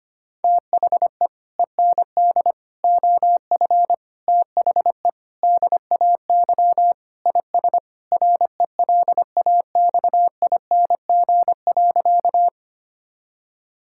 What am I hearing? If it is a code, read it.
Morse code, heard as a keyed tone, about 25 wpm, 700 Hz. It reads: THE END OF THE DAY IS RELAXING.
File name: practice / 25wpm / Sentences with 3-Word Phrases - 25wpm